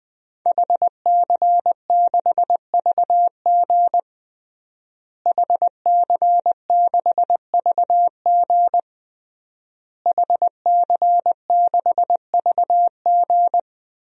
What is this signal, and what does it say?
Morse code, heard as a keyed tone, about 20 wpm, 700 Hz. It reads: HC6VG HC6VG HC6VG
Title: DXCC-100 Call Signs - 5 Characters - Repeated 3x - 20wpm